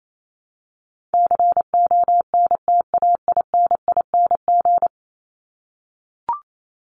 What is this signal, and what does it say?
Morse code, heard as a keyed tone, about 28 wpm, 700 Hz. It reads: CONTAINING
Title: ING Suffix - Top 100 Words with Base Words 28wpm